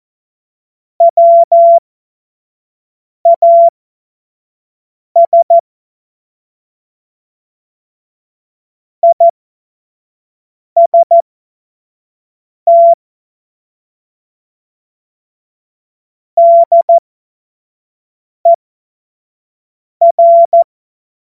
Morse code, heard as a keyed tone, about 14 wpm, spaced as for 5 wpm, 680 Hz. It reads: WAS IST DER
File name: Morsekurs / Witze / Witz 29 April 2025 25 F70